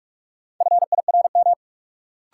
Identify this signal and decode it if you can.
Morse code, heard as a keyed tone, about 43 wpm, 700 Hz. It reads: FIRM
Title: Sentences from Top 1400 Words - New Word Drill - 45wpm